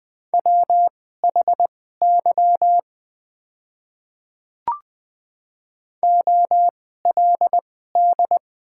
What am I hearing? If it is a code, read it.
Morse code, heard as a keyed tone, about 20 wpm, spaced as for 14 wpm, 700 Hz. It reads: WHY OLD
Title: 2x Character Spacing - Top 200 Words - 20wpm